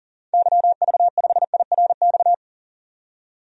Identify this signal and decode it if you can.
Morse code, heard as a keyed tone, about 40 wpm, 700 Hz. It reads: YV5IRX
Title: DXCC-100 Call Signs - 6 Characters - 40wpm